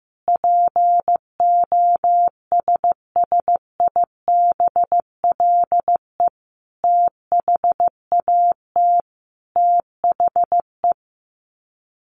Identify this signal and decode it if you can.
Morse code, heard as a keyed tone, about 15 wpm, 700 Hz. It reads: POSSIBLE THAT THE
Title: Sets of 5 Words Encore2 15wpm